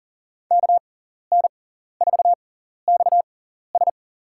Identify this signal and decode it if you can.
Morse code, heard as a keyed tone, about 38 wpm, spaced as for 14 wpm, 700 Hz. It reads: KN4XS
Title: USA Call Signs - 5 Characters - 6x Character Spacing - 40wpm